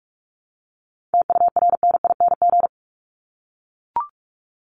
Morse code, heard as a keyed tone, about 44 wpm, 700 Hz. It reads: TURNING
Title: ING Suffix - Top 100 Words with Base Words 45wpm